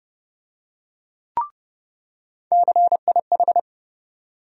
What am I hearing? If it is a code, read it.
Morse code, heard as a keyed tone, about 30 wpm, 700 Hz. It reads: CIH
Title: USA Call Sign Postfixes - 3 Letters 30wpm